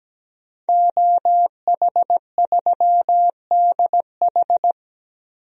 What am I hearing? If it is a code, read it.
Morse code, heard as a keyed tone, about 17 wpm, 700 Hz. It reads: OH3DH
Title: DXCC-100 Call Signs - 5 Characters - 17wpm